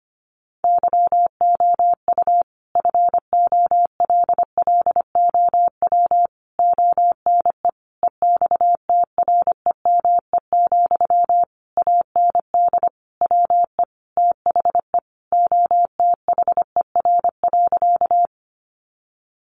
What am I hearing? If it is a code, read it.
Morse code, heard as a keyed tone, about 25 wpm, 700 Hz. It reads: YOU FOLLOW ONE EXTREME, AND WE THE OTHER.
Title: Sentences from Top 1300 Words 25wpm